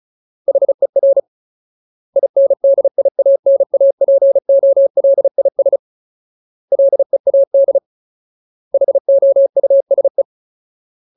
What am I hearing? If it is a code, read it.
Morse code, heard as a keyed tone, about 34 wpm, 550 Hz. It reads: HER INDIANAPOLIS LEAD HOUSE
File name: Ultimate Word List / 35WPM Ultimate